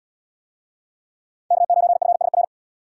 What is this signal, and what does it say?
Morse code, heard as a keyed tone, about 70 wpm, 700 Hz. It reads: K9FSR